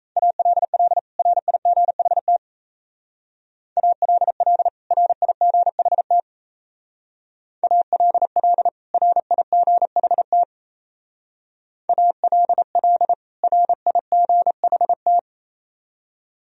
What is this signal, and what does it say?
Morse code, heard as a keyed tone, about 37 wpm, 700 Hz. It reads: ALL RIGHT ALL RIGHT ALL RIGHT ALL RIGHT